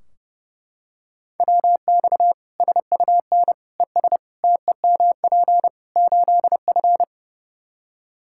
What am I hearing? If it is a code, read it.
Morse code, heard as a keyed tone, about 30 wpm, 700 Hz. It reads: WX SUN ES TEMP 8F
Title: Practice QSO - Exchange 2 - Info - 30wpm